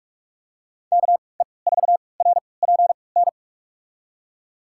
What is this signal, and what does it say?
Morse code, heard as a keyed tone, about 43 wpm, spaced as for 25 wpm, 700 Hz. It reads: KE4RPN
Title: USA Call Signs - 6 Characters - 3x Character Spacing - 45wpm